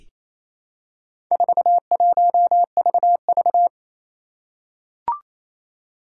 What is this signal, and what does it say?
Morse code, heard as a keyed tone, about 28 wpm, 700 Hz. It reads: 41VV
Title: Four Challenging Letter-Number Combinations 28wpm